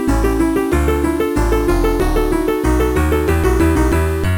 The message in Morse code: none